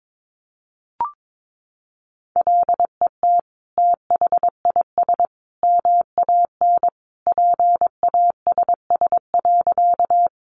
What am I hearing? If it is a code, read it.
Morse code, heard as a keyed tone, about 22 wpm, 700 Hz. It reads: LET THIS MAN PASS.
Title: Sentences from Top 400 Words 22wpm